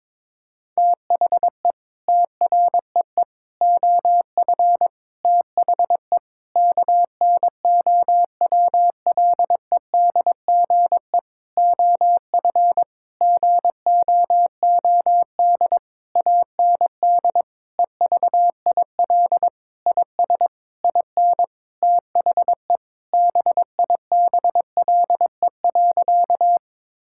Morse code, heard as a keyed tone, about 22 wpm, 700 Hz. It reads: THE TREE OF THE KNOWLEDGE OF GOOD AND EVIL IS IN THE BIBLE.
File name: Sentences with 9-Word Phrases - 22wpm